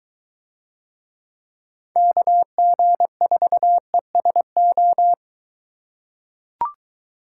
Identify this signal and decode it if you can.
Morse code, heard as a keyed tone, about 21 wpm, 700 Hz. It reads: KG4ESO